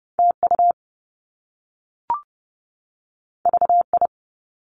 Morse code, heard as a keyed tone, about 30 wpm, 700 Hz. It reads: TU VI
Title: Top Two Letter Combinations in Words 30wpm